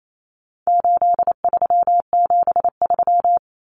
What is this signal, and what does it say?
Morse code, heard as a keyed tone, about 28 wpm, 700 Hz. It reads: 8373
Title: Numbers - 4-Digit 28wpm